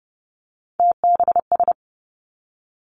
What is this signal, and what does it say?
Morse code, heard as a keyed tone, about 30 wpm, 700 Hz. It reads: TBS
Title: Three Challenging Letter-Number Combinations 30wpm